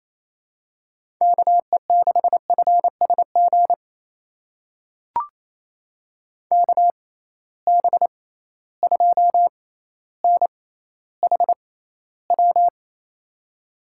Morse code, heard as a keyed tone, about 28 wpm, spaced as for 14 wpm, 700 Hz. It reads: KE6FSG KB2NHW
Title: USA Call Signs - 6 Characters - 6x Character Spacing - 28wpm